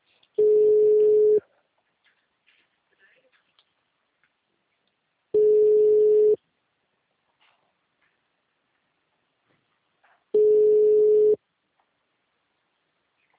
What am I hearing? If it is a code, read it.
Ringback tone (Europe, 425 Hz), with speech.
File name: out- -2008- - - .105